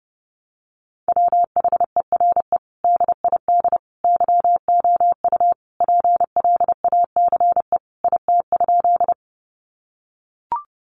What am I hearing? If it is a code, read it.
Morse code, heard as a keyed tone, about 30 wpm, 700 Hz. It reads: WHERE DID YOU PLACE IT?
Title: Sentences from Top 200 Words 30wpm